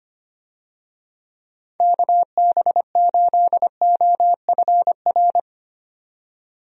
Morse code, heard as a keyed tone, about 25 wpm, 700 Hz. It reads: KB8OFR
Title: USA Call Signs - 6 Characters - 25wpm